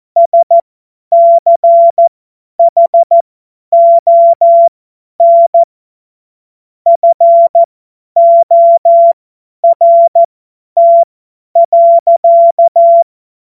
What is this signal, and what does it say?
Morse code, heard as a keyed tone, about 14 wpm, spaced as for 10 wpm, 680 Hz. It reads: SCHON FORT.
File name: Morsekurs / Witze / Witz 04. April 2023 50 F70